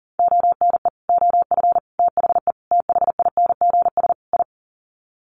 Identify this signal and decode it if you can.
Morse code, heard as a keyed tone, about 40 wpm, 700 Hz. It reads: ONE OF THE THINGS I
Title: Sets of 5 Words 40wpm